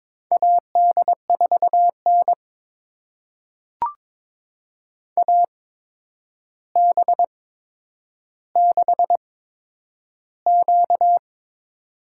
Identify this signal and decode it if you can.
Morse code, heard as a keyed tone, about 21 wpm, spaced as for 9 wpm, 700 Hz. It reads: AD4N AB6Q
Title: USA Call Signs - 4 Characters - 8x Character Spacing - 22wpm